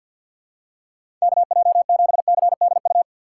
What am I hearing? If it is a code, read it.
Morse code, heard as a keyed tone, about 53 wpm, 700 Hz. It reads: KJ7CDU
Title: USA Call Signs - 6 Characters - 8x Character Spacing - 50wpm